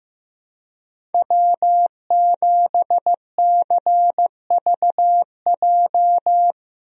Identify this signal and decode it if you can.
Morse code, heard as a keyed tone, about 15 wpm, 700 Hz. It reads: W7CVJ
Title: USA Call Signs - 5 Characters - Repeated 2x - 15wpm